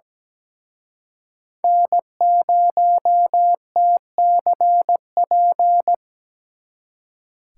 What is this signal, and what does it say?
Morse code, heard as a keyed tone, about 20 wpm, 700 Hz. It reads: N0TCP
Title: Speed Racing - USA 1x3 Call Signs 17wpm